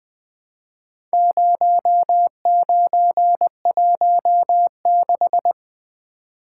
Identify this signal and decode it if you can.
Morse code, heard as a keyed tone, about 20 wpm, 700 Hz. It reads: 0916
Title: Four Letter-Number Combinations - 20wpm